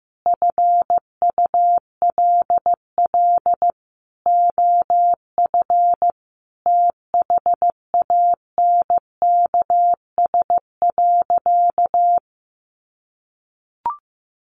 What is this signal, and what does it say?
Morse code, heard as a keyed tone, about 14 wpm, 700 Hz. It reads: FULL OF THANKS.
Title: Sentences from Top 1200 Words 15wpm